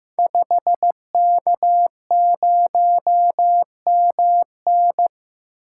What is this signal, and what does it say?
Morse code, heard as a keyed tone, about 15 wpm, 700 Hz. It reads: 5K0MN
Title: Code Groups - 5 Letter:Numbers - 15wpm